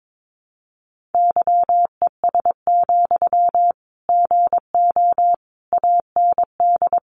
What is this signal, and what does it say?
Morse code, heard as a keyed tone, about 22 wpm, 700 Hz. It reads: YES, GO AND
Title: Sentences from Top 500 Words 22wpm